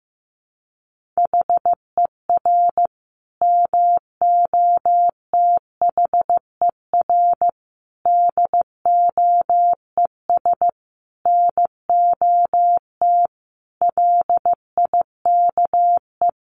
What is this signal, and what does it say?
Morse code, heard as a keyed tone, about 15 wpm, 700 Hz. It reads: HER MOTHER DOES NOT LIKE
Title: Sentences from Top 1400 Words 15wpm